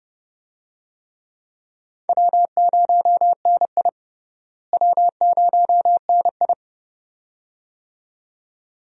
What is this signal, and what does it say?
Morse code, heard as a keyed tone, about 30 wpm, 700 Hz. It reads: W0NI W0NI
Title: USA Call Signs - 4 Characters - Repeated 2x - 30wpm